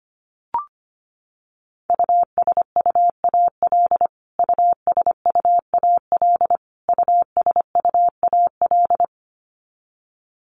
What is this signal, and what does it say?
Morse code, heard as a keyed tone, about 25 wpm, 700 Hz. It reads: USUAL USUAL USUAL
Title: Top 500 Words Repeated 25wpm